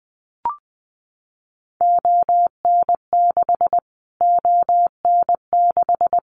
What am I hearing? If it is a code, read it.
Morse code, heard as a keyed tone, about 20 wpm, 700 Hz. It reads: ON6 ON6
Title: Callsign Prefixes for Top 25 DXCC - Repeated 3x - 20wpm